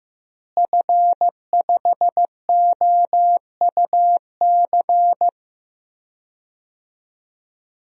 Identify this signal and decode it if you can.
Morse code, heard as a keyed tone, about 15 wpm, 700 Hz. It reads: F5OUC
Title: DXCC-100 Call Signs - 5 Characters - Repeated 2x - 15wpm